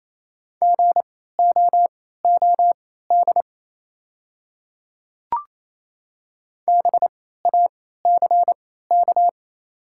Morse code, heard as a keyed tone, about 28 wpm, spaced as for 16 wpm, 700 Hz. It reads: GOOD BACK
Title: 3x Character Spacing - Top 200 Words - 28wpm